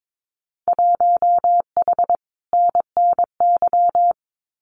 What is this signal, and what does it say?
Morse code, heard as a keyed tone, about 22 wpm, 700 Hz. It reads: 1H NNY
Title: Winter Field Day Exchanges 22wpm